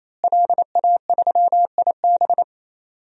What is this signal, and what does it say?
Morse code, heard as a keyed tone, about 28 wpm, 700 Hz. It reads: LA3IB
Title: DXCC-100 Call Signs - 5 Characters - Repeated 3x - 28wpm